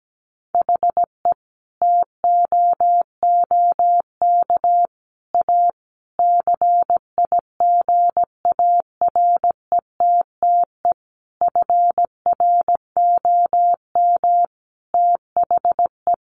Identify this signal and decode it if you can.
Morse code, heard as a keyed tone, about 17 wpm, 700 Hz. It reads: HE TOOK A CIGARETTE FROM THE
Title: Sentences from Top 1000 Words 17wpm